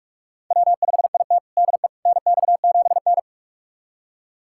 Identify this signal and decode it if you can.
Morse code, heard as a keyed tone, about 46 wpm, 700 Hz. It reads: W5IT DE NX7N
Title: USA Callsign DE Callsign - 4 Characters - 45wpm